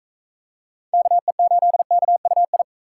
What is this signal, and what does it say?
Morse code, heard as a keyed tone, about 38 wpm, 700 Hz. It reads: KE8KUI